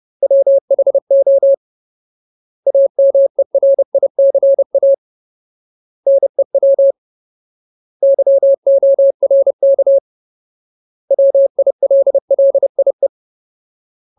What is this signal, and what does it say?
Morse code, heard as a keyed tone, about 30 wpm, 550 Hz. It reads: WHO AMERICA NEW YORK WILLIE
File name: Ultimate Word List / 30WPM Ultimate